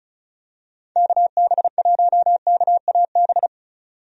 Morse code, heard as a keyed tone, about 36 wpm, 700 Hz. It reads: KB1KAB
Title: USA Call Signs - 6 Characters - 35wpm